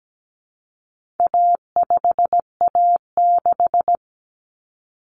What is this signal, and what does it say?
Morse code, heard as a keyed tone, about 17 wpm, 700 Hz. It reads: A5A6
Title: Common Cut Numbers - 4-Digit 17wpm